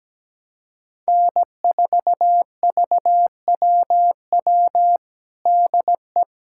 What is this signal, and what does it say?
Morse code, heard as a keyed tone, about 17 wpm, 700 Hz. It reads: N4VWW DE
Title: USA Callsign DE Callsign - 5 Characters - 17wpm